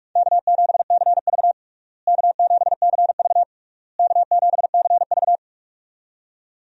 Morse code, heard as a keyed tone, about 41 wpm, 700 Hz. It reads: K7CV K7CV K7CV
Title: USA Call Signs - 4 Characters - Repeated 3x - 45wpm